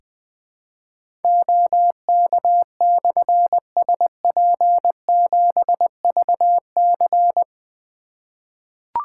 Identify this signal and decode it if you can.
Morse code, heard as a keyed tone, about 20 wpm, 700 Hz. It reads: OK/SP7VC
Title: Difficult Call Signs 20wpm